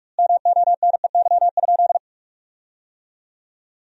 Morse code, heard as a keyed tone, about 43 wpm, 700 Hz. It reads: MONEY?
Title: 2x Word-Spacing - Sentences from Top 400 Words - 45wpm